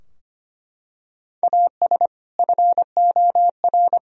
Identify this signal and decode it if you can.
Morse code, heard as a keyed tone, about 25 wpm, 700 Hz. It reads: AS FOR
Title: Sentences from Top 1000 Words - Review Part 3 - 25wpm